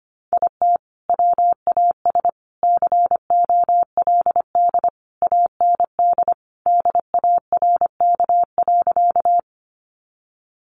Morse code, heard as a keyed tone, about 25 wpm, 700 Hz. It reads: IT WAS COLD AND DARK.
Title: Sentences from Top 500 Words 25wpm